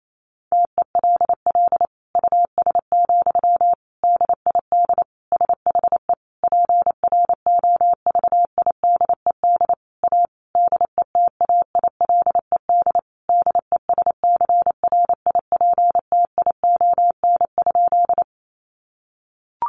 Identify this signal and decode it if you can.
Morse code, heard as a keyed tone, about 28 wpm, 700 Hz. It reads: TELL US, DID SHE PROVIDED A DETAILED DESCRIPTION?
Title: Sentences from Top 1500 Words 28wpm